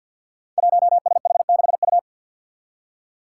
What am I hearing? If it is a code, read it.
Morse code, heard as a keyed tone, about 52 wpm, 700 Hz. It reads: 1SH6U
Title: Five Challenging Letter-Number Combinations 50wpm